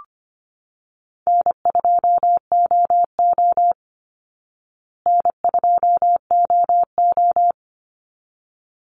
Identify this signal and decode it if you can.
Morse code, heard as a keyed tone, about 25 wpm, 700 Hz. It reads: N2OO N2OO
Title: Easy Call Signs Repeated 2x - 25wpm